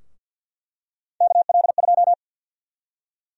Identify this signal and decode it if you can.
Morse code, heard as a keyed tone, about 48 wpm, 700 Hz. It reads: KL2
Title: USA Call Sign Prefixes - 50wpm